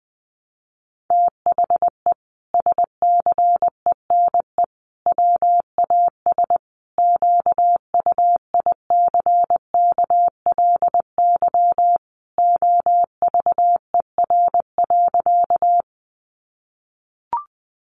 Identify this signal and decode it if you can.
Morse code, heard as a keyed tone, about 20 wpm, 700 Hz. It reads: THE SCENE WAS QUICKLY OVER.
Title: Sentences from Top 1100 Words 20wpm